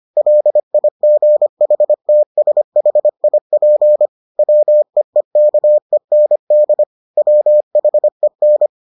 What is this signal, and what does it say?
Morse code, heard as a keyed tone, about 25 wpm, 600 Hz. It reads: LIGHTSHIP WEEKEND WHEN